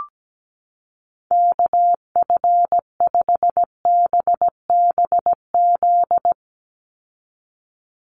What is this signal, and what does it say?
Morse code, heard as a keyed tone, about 17 wpm, 700 Hz. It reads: KF5BBZ
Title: USA Call Signs 17wpm